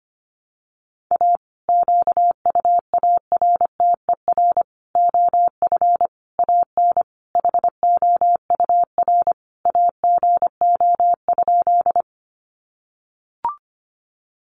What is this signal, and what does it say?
Morse code, heard as a keyed tone, about 25 wpm, 700 Hz. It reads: A QUARTER OF AN HOUR AGO?
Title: Sentences from Top 1300 Words 25wpm